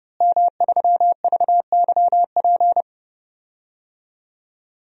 Morse code, heard as a keyed tone, about 31 wpm, 700 Hz. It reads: M3VYP
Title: DXCC-100 Call Signs - 5 Characters - Repeated 3x - 30wpm